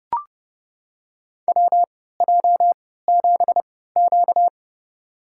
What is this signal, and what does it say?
Morse code, heard as a keyed tone, about 30 wpm, spaced as for 17 wpm, 700 Hz. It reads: WJ7Q
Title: USA Call Signs - 4 Characters - 3x Character Spacing - 30wpm